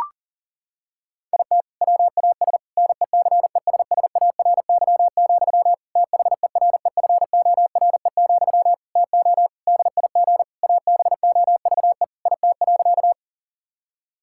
Morse code, heard as a keyed tone, about 40 wpm, 700 Hz. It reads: IT WAS NECESSARY, THEREFORE, TO DIG ABOVE IT.
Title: Sentences from Top 1600 Words 40wpm